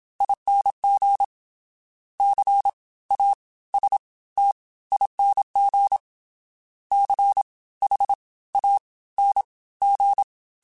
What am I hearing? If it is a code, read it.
Morse code, heard as a keyed tone, about 27 wpm, spaced as for 18 wpm, 800 Hz. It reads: ING CASTING CHANG
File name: Special INGs / Special INGs-mp3s / Fast ing @25